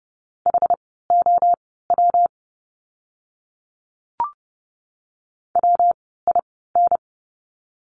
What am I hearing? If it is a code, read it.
Morse code, heard as a keyed tone, about 30 wpm, spaced as for 17 wpm, 700 Hz. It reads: HOW WIN
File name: course / 30:10wpm / 115 Lesson 19f - Letter w - Words 30wpm